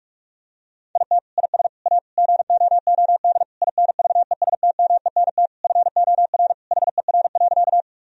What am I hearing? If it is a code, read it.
Morse code, heard as a keyed tone, about 46 wpm, 700 Hz. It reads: IT IS A GOOD INVESTMENT FOR HER.
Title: Sentences from Top 1000 Words - Review Part 4 - 45wpm